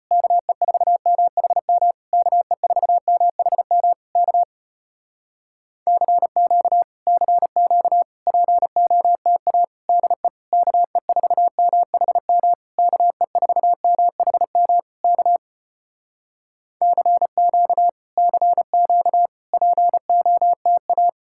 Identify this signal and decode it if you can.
Morse code, heard as a keyed tone, about 35 wpm, 700 Hz. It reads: KE4MHM KE4MHM K CQ CQ POTA DE KE4MHM KE4MHM K CQ CQ POTA